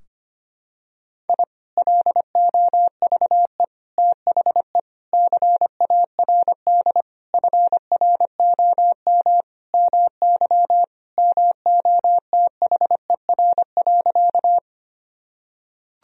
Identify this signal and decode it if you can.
Morse code, heard as a keyed tone, about 25 wpm, 700 Hz. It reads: I LOVE THE CARD FROM MY MOTHER.
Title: Sentences from Top 1000 Words - Review Part 4 - 25wpm